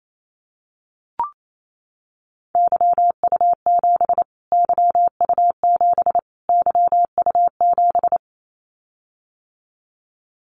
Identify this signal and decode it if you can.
Morse code, heard as a keyed tone, about 28 wpm, 700 Hz. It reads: YU7 YU7 YU7
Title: Callsign Prefixes for Top 25 DXCC - Repeated 3x - 28wpm